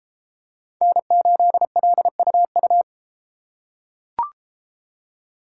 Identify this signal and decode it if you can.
Morse code, heard as a keyed tone, about 32 wpm, 700 Hz. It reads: N8LUU